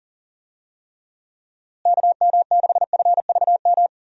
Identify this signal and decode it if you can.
Morse code, heard as a keyed tone, about 40 wpm, 700 Hz. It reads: KM6FVM
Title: USA Call Signs - 6 Characters - Repeated 2x - 40wpm